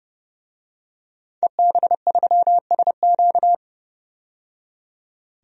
Morse code, heard as a keyed tone, about 29 wpm, 700 Hz. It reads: EB3SQ
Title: DXCC-100 Call Signs - 5 Characters - 30wpm